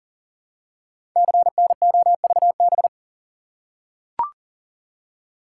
Morse code, heard as a keyed tone, about 39 wpm, 700 Hz. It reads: CNOVB
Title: Code Groups - 5 Letters - 40wpm